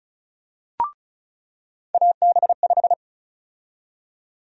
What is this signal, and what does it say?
Morse code, heard as a keyed tone, about 34 wpm, 700 Hz. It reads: AB5
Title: USA Call Sign Prefixes - 35wpm